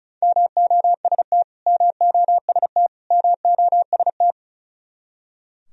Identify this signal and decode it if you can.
Morse code, heard as a keyed tone, about 34 wpm, 700 Hz. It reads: MOST MOST MOST
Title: Top 100 Words Repeated 35wpm